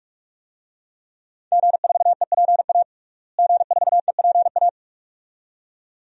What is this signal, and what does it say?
Morse code, heard as a keyed tone, about 41 wpm, 700 Hz. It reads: G4EPA G4EPA
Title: DXCC-100 Call Signs - 5 Characters - Repeated 2x - 45wpm